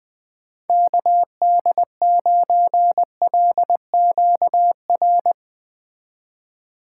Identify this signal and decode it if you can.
Morse code, heard as a keyed tone, about 20 wpm, 700 Hz. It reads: KD9LQR
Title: Speed Racing - USA 2x3 Call Signs 17wpm